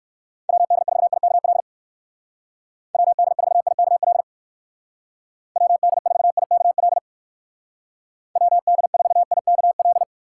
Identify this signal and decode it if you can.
Morse code, heard as a keyed tone, about 59 wpm, 700 Hz. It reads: WD4IKL WD4IKL WD4IKL WD4IKL